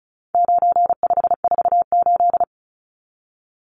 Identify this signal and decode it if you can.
Morse code, heard as a keyed tone, about 34 wpm, 700 Hz. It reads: 9548
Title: Numbers - 4-Digit 35wpm